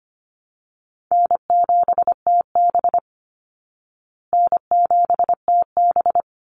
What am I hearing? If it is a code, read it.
Morse code, heard as a keyed tone, about 25 wpm, 700 Hz. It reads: N7TB N7TB
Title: Call Signs Repeated 2x - 25wpm